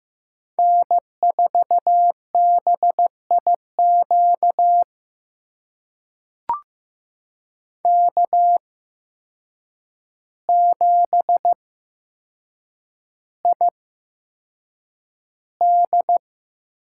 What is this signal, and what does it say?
Morse code, heard as a keyed tone, about 15 wpm, spaced as for 7 wpm, 700 Hz. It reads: N4BIQ K7ID
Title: USA Call Signs - 5 Characters - 8x Character Spacing - 15wpm